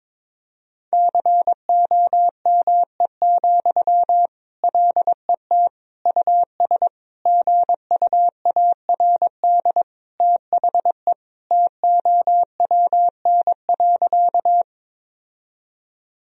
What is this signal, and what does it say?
Morse code, heard as a keyed tone, about 21 wpm, 700 Hz. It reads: COME, LET US GUARD THE TOWN.
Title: Sentences from Top 1000 Words - Review Part 4 - 22wpm